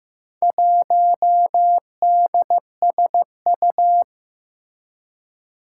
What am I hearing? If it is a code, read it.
Morse code, heard as a keyed tone, about 15 wpm, 700 Hz. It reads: 1DSU
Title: Four Challenging Letter-Number Combinations 15wpm